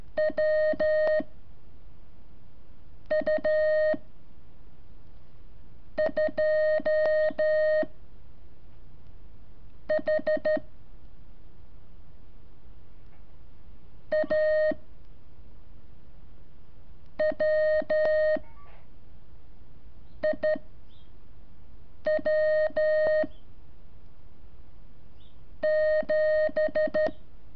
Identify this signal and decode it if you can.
Morse code, heard as a keyed tone, about 13 wpm, 620 Hz. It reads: W U 2 H A W I W 7
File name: curso / ejercicio 27